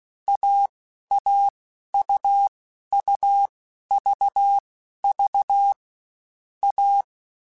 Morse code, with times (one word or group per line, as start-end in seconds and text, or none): 0.3-0.7 A
1.1-1.5 A
1.9-2.5 U
2.9-3.5 U
3.9-4.6 V
5.0-5.7 V
6.6-7.0 A